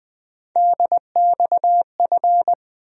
Morse code, heard as a keyed tone, about 20 wpm, 700 Hz. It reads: DXF
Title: USA Call Sign Postfixes - 3 Letters 20wpm